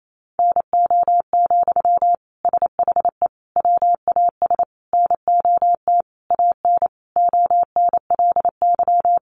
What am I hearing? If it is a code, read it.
Morse code, heard as a keyed tone, about 28 wpm, 700 Hz. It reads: NO, SHE WAS NOT AN ONLY
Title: Sentences from Top 800 Words 28wpm